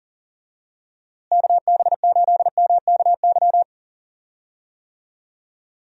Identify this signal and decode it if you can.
Morse code, heard as a keyed tone, about 38 wpm, 700 Hz. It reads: KB8MKY
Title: USA Call Signs - 6 Characters - Repeated 3x - 40wpm